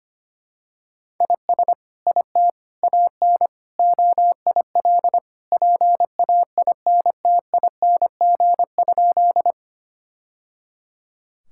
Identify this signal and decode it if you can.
Morse code, heard as a keyed tone, about 25 wpm, 700 Hz. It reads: IS IT AN OIL PAINTING?
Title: Sentences from Top 1000 Words - Review Part 6 - 25wpm